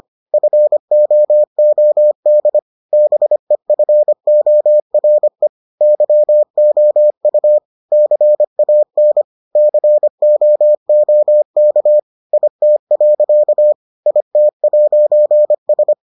Morse code, heard as a keyed tone, about 25 wpm, 600 Hz. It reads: FOOD BEFORE YOU CAN COOK IT. IT'S